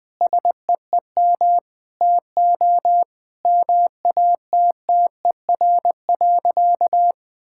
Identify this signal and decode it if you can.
Morse code, heard as a keyed tone, about 20 wpm, 700 Hz. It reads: SEEM TO MATTER.
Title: Sentences with 4-Word Phrases - 20wpm